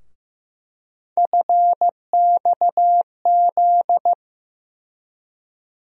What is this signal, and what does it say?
Morse code, heard as a keyed tone, about 15 wpm, 700 Hz. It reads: FXZ
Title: USA Call Sign Postfixes - 3 Letters 15wpm